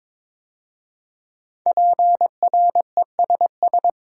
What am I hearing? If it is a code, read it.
Morse code, heard as a keyed tone, about 22 wpm, 700 Hz. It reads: PRESS
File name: 5 Letter Words - First 3 Letters Given - 22wpm